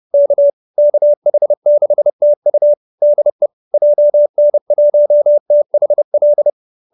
Morse code, heard as a keyed tone, about 30 wpm, 590 Hz. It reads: K KH6TU DE JN1THL